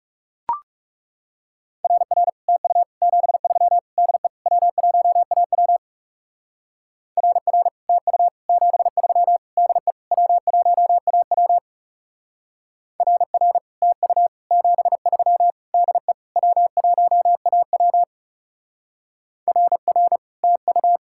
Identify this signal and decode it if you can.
Morse code, heard as a keyed tone, about 35 wpm, 700 Hz. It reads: RR TU 73 DE W1AW RR TU 73 DE W1AW RR TU 73 DE W1AW RR TU
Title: Speed-Racing Common QSO Phrases Encore - 30wpm